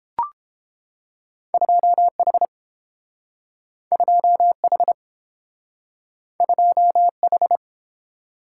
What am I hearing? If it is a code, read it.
Morse code, heard as a keyed tone, about 30 wpm, 700 Hz. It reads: 2H 2H 2H